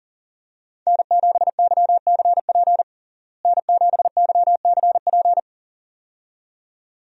Morse code, heard as a keyed tone, about 40 wpm, 700 Hz. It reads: N7YCP N7YCP
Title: USA Call Signs - 5 Characters - Repeated 2x - 40wpm